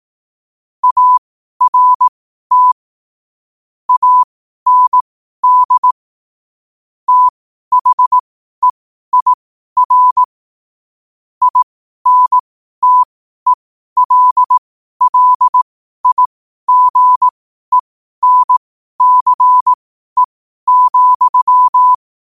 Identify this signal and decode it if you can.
Morse code, heard as a keyed tone, about 18 wpm, spaced as for 12 wpm, 1000 Hz. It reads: ART AND THEIR INTELLIGENCE,
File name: qotdfiles / 12wpm